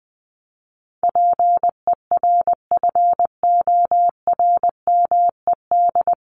Morse code, heard as a keyed tone, about 20 wpm, 700 Hz. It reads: PERFORMED
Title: ED Suffix - Top 100 Words with Base Words 20wpm